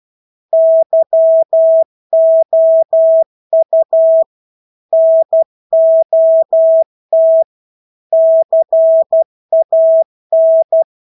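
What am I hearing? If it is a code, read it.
Morse code, heard as a keyed tone, about 12 wpm, 650 Hz. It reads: YOU NOT CAN
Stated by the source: Morse practice of English words at 12 wpm.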